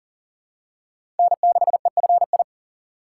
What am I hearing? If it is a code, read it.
Morse code, heard as a keyed tone, about 40 wpm, 700 Hz. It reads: N6EFI
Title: USA Call Signs - 5 Characters - 2x Character Spacing - 40wpm